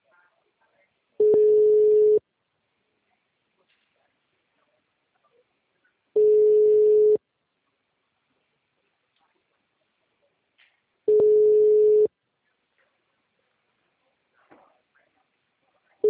Ringback tone (Europe, 425 Hz).